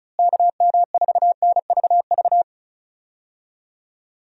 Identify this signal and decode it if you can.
Morse code, heard as a keyed tone, about 34 wpm, 700 Hz. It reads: KM4NVV
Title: USA Call Signs - 6 Characters - Repeated 2x - 35wpm